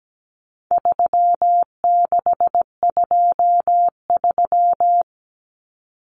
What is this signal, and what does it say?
Morse code, heard as a keyed tone, about 17 wpm, 700 Hz. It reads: 3623
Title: Numbers - 4-Digit 17wpm